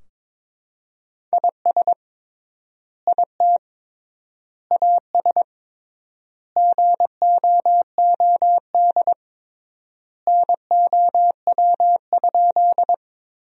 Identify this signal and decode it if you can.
Morse code, heard as a keyed tone, about 22 wpm, 700 Hz. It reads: IS IT AS GOOD NOW?
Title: 3x Word-Spacing - Sentences from Top 200 Words - 22wpm